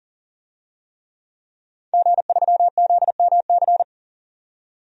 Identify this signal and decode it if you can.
Morse code, heard as a keyed tone, about 40 wpm, 700 Hz. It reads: G3ZMC
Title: DXCC-100 Call Signs - 5 Characters - 40wpm